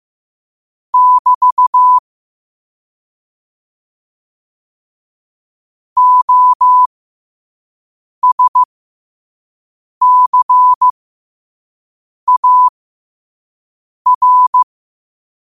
Morse code, heard as a keyed tone, about 15 wpm, spaced as for 5 wpm, 1000 Hz. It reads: <BT> OSCAR
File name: qotdfiles / 5wpm